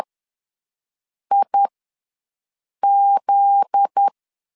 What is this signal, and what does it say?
Morse code, heard as a keyed tone, about 10 wpm, spaced as for 5 wpm, 800 Hz. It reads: IZ
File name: LEZIONI / 9 2867 ws 10wpm 4215